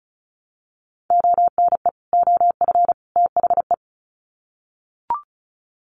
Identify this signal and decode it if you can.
Morse code, heard as a keyed tone, about 34 wpm, 700 Hz. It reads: ONE OF THE
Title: Sets of 3 Words 35wpm